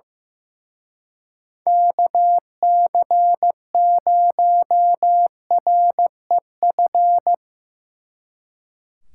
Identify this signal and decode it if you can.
Morse code, heard as a keyed tone, about 18 wpm, 700 Hz. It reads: KC0REF
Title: Speed Racing - USA 2x3 Call Signs 15wpm